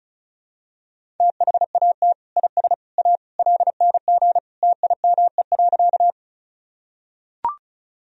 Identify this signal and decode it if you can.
Morse code, heard as a keyed tone, about 36 wpm, 700 Hz. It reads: THAT IS A LNG TIME.
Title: Sentences from Top 100 Words and Abbreviations 35wpm